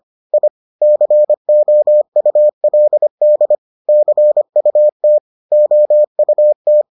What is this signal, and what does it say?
Morse code, heard as a keyed tone, about 25 wpm, 600 Hz. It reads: I COULD CUT OUT